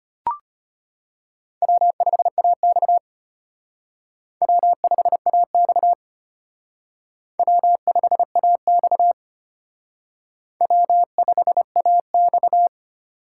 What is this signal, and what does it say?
Morse code, heard as a keyed tone, about 32 wpm, 700 Hz. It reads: W5AX W5AX W5AX W5AX